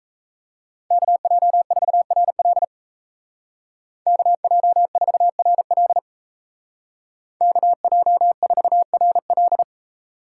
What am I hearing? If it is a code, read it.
Morse code, heard as a keyed tone, about 36 wpm, 700 Hz. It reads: KJ4RL KJ4RL KJ4RL